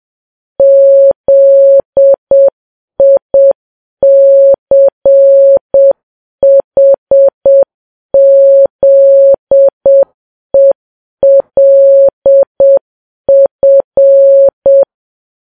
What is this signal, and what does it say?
Morse code, heard as a keyed tone, about 7 wpm, 550 Hz. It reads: ZICHZELF